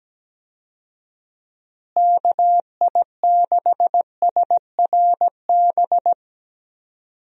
Morse code, heard as a keyed tone, about 17 wpm, 700 Hz. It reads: KI6SRB
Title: USA Call Signs - 6 Characters - Repeated 3x - 17wpm